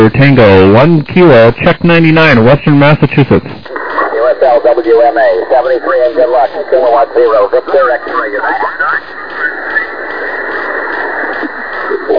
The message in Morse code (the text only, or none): none